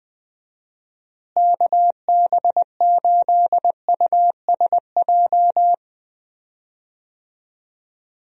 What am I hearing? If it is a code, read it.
Morse code, heard as a keyed tone, about 20 wpm, 700 Hz. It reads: KB8USJ
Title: USA Call Signs - 6 Characters - Repeated 3x - 20wpm